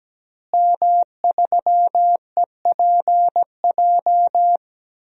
Morse code, heard as a keyed tone, about 17 wpm, 700 Hz. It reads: M3EPJ
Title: DXCC-100 Call Signs - 5 Characters - Repeated 2x - 17wpm